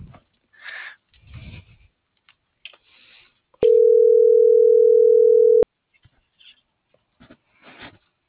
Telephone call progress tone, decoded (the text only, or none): ringback tone